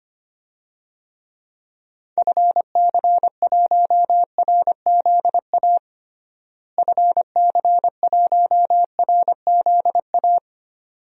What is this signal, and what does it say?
Morse code, heard as a keyed tone, about 25 wpm, 700 Hz. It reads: FC1RZA FC1RZA
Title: DXCC-100 Call Signs - 6 Characters - Repeated 2x - 25wpm